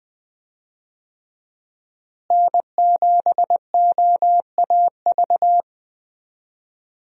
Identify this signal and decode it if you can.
Morse code, heard as a keyed tone, about 20 wpm, 700 Hz. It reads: N7OAV
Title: USA Call Signs - 5 Characters - Repeated 2x - 20wpm